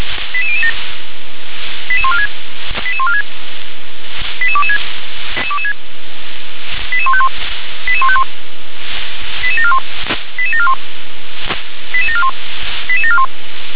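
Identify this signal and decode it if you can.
Five-tone selective call (ZVEI), heard as quick runs of five tones, tones of about 70 ms. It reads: 88007 88146 88146 88226 88151 88151 88741 88741 88741 88741
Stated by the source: Five-tone ZVEI call received on an SDR.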